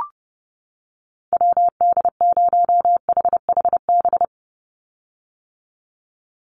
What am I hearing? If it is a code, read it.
Morse code, heard as a keyed tone, about 30 wpm, 700 Hz. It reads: WD0HHB
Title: USA Call Signs 30wpm